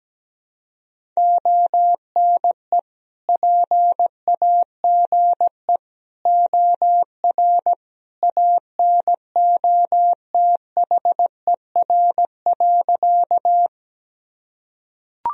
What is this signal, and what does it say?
Morse code, heard as a keyed tone, about 17 wpm, 700 Hz. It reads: ONE PAGE OR ANOTHER.
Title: Sentences from Top 1000 Words - Review Part 1 - 17wpm